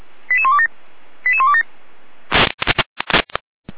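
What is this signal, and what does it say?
Five-tone selective call (ZVEI), heard as quick runs of five tones, tones of about 70 ms. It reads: 88127 88127
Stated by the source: ZVEI five-tone selective call received on an SDR, recorded by the Feuerwehr Orlamuende fire brigade.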